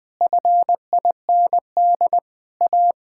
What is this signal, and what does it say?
Morse code, heard as a keyed tone, about 20 wpm, 700 Hz. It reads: FIND A
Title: Sentences with 8-Word Phrases - 20wpm